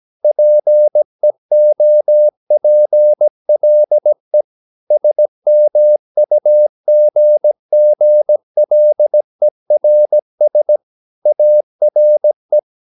Morse code, heard as a keyed tone, about 17 wpm, 600 Hz. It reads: PEOPLE SMUGGLERS ARE
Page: Romanen Never af Ken Follet på engelsk.